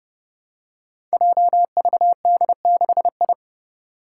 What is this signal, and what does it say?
Morse code, heard as a keyed tone, about 30 wpm, 700 Hz. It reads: JVD6I
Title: Five Challenging Letter-Number Combinations 30wpm